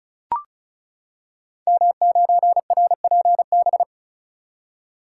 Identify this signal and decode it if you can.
Morse code, heard as a keyed tone, about 36 wpm, 690 Hz. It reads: M9RPB
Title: Code Groups - 5 Letter:Numbers - 35wpm